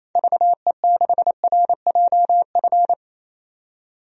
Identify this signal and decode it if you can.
Morse code, heard as a keyed tone, about 28 wpm, 700 Hz. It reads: VE6RJF
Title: DXCC-100 Call Signs - 6 Characters - 28wpm